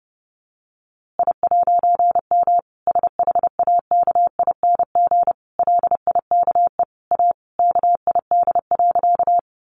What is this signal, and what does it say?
Morse code, heard as a keyed tone, about 30 wpm, 700 Hz. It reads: I'M SHAKING LIKE A KID.
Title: A Sound Of Thunder 30wpm